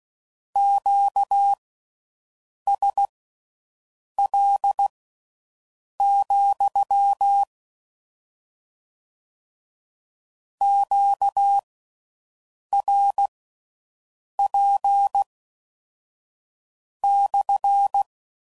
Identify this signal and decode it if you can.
Morse code, heard as a keyed tone, about 16 wpm, spaced as for 6 wpm, 800 Hz. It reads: QSL, QRP /